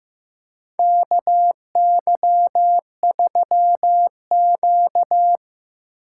Morse code, heard as a keyed tone, about 15 wpm, 700 Hz. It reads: KY3Q
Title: USA Call Signs - 2x Character Spacing - 15wpm